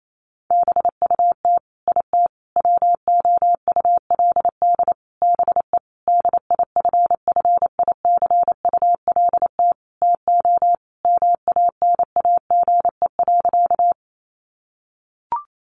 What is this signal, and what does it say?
Morse code, heard as a keyed tone, about 28 wpm, 700 Hz. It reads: BUT IT WOULD BE DIFFICULT TO MANAGE.
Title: Sentences from Top 1400 Words 28wpm